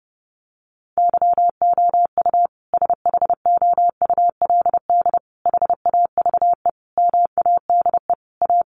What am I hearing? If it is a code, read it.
Morse code, heard as a keyed tone, about 30 wpm, 700 Hz. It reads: YOU SHOULD HAVE MADE A
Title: Sentences from Top 1200 Words 30wpm